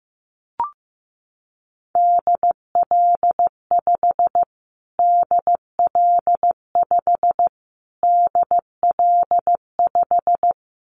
Morse code, heard as a keyed tone, about 15 wpm, 700 Hz. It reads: DL5 DL5 DL5
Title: Callsign Prefixes for Top 25 DXCC - Repeated 3x - 15wpm